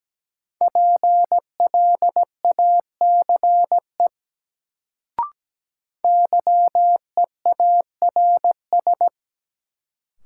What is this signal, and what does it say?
Morse code, heard as a keyed tone, about 17 wpm, 700 Hz. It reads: PLACE YEARS
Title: Sentences from Top 200 Words - New Word Drill - 17wpm